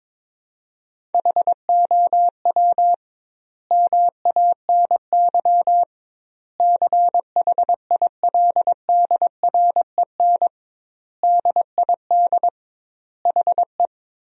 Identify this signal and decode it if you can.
Morse code, heard as a keyed tone, about 23 wpm, 700 Hz. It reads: HOW MANY CHILDREN DID HE
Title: 2x Word-Spacing - Sentences from Top 400 Words - 22wpm